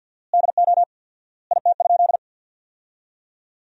Morse code, heard as a keyed tone, about 50 wpm, 700 Hz. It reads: DO IT?
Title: 4x Word-Spacing - Sentences from Top 200 Words - 50wpm